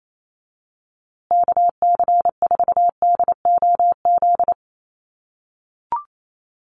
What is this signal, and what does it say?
Morse code, heard as a keyed tone, about 28 wpm, 700 Hz. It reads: KC4DOZ
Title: USA Call Signs Repeated 3x 28wpm